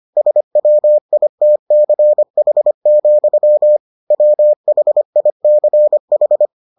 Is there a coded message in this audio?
Yes: Morse code from 0.2 to 6.5 s, heard as a keyed tone, about 25 wpm, 600 Hz. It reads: SWITCH, WHICH